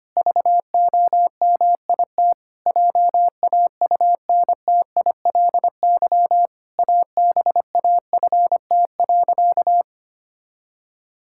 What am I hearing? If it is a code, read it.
Morse code, heard as a keyed tone, about 25 wpm, 700 Hz. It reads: VOMIT JAUNTILY ABAFT.
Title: Pangram Challenge v2 - 25wpm